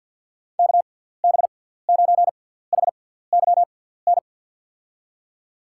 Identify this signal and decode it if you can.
Morse code, heard as a keyed tone, about 48 wpm, spaced as for 17 wpm, 700 Hz. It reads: KB9HYN
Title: USA Call Signs - 6 Characters - 6x Character Spacing - 50wpm